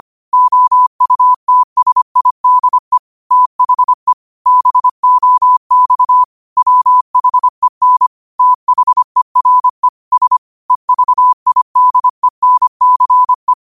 Morse code, heard as a keyed tone, about 25 wpm, 1000 Hz. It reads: OUTSIDE THE BOX WHEN THERE S EVIDENCE
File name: qotdfiles / 25wpm